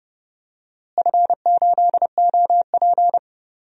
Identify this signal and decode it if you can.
Morse code, heard as a keyed tone, about 31 wpm, 700 Hz. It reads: F8OP
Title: DXCC-100 Call Signs - 4 Characters - Repeated 3x - 30wpm